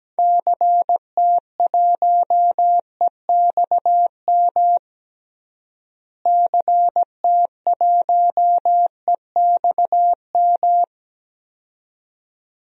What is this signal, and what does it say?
Morse code, heard as a keyed tone, about 17 wpm, 700 Hz. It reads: CT1EXM CT1EXM
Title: DXCC-100 Call Signs - 6 Characters - Repeated 2x - 17wpm